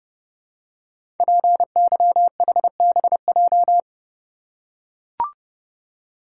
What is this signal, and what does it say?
Morse code, heard as a keyed tone, about 30 wpm, 700 Hz. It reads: PYHBJ
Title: Code Groups - 5 Letters - 30wpm